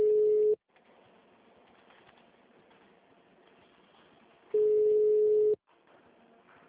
Ringback tone (Europe, 425 Hz).
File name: out- -2007- - - .177